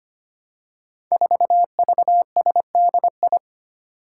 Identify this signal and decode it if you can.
Morse code, heard as a keyed tone, about 25 wpm, 700 Hz. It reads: 4VSDI
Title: Five Challenging Letter-Number Combinations 25wpm